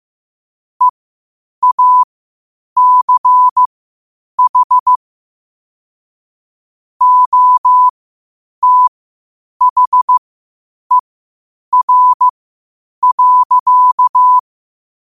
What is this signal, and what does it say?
Morse code, heard as a keyed tone, about 15 wpm, spaced as for 8 wpm, 1000 Hz. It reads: EACH OTHER.
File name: qotdfiles / 8wpm